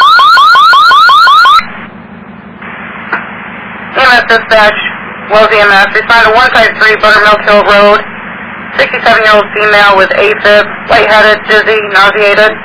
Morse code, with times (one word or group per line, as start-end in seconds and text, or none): none